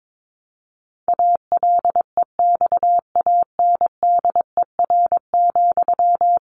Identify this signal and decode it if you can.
Morse code, heard as a keyed tone, about 22 wpm, 700 Hz. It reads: ALEXANDER,
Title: A Sound Of Thunder 22wpm